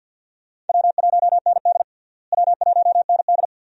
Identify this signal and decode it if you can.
Morse code, heard as a keyed tone, about 47 wpm, 700 Hz. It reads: W1ND W1ND
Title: USA Call Signs - 4 Characters - Repeated 2x - 50wpm